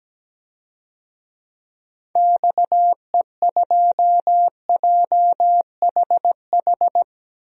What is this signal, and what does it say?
Morse code, heard as a keyed tone, about 17 wpm, 700 Hz. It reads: XE2JHH
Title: DXCC-100 Call Signs - 6 Characters - 17wpm